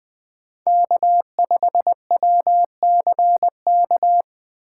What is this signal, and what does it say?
Morse code, heard as a keyed tone, about 20 wpm, 700 Hz. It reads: K5WCK
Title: USA Call Signs - 5 Characters - 4x Character Spacing - 20wpm